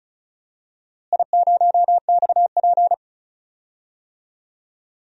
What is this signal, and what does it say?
Morse code, heard as a keyed tone, about 34 wpm, 700 Hz. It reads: I0XP
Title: DXCC-100 Call Signs - 4 Characters - Repeated 3x - 35wpm